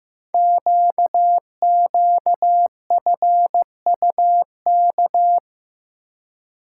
Morse code, heard as a keyed tone, about 15 wpm, 700 Hz. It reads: QQFUK
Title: Code Groups - 5 Letters - 15wpm